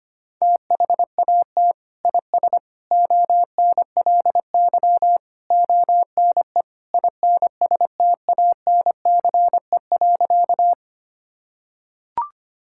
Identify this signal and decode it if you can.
Morse code, heard as a keyed tone, about 25 wpm, 700 Hz. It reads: THAT IS ONLY ONE INSTANCE.
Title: Sentences from Top 1000 Words - Review Part 5 - 25wpm